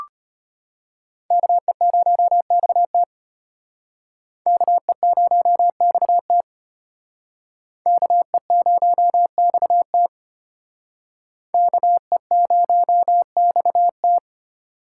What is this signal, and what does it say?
Morse code, heard as a keyed tone, about 33 wpm, 700 Hz. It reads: KE0XT KE0XT KE0XT KE0XT